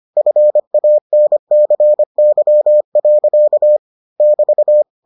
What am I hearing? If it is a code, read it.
Morse code, heard as a keyed tone, about 25 wpm, 600 Hz. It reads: FANCY. <BT>